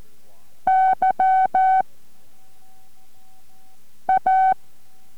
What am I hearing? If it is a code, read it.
Morse code, heard as a keyed tone, about 14 wpm, 760 Hz. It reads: Y A